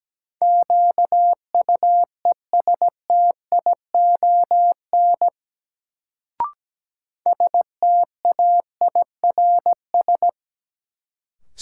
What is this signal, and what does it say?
Morse code, heard as a keyed tone, about 17 wpm, 700 Hz. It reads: QUESTION STAIRS
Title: Sets of 5 Words Encore - New Word Drill - 17wpm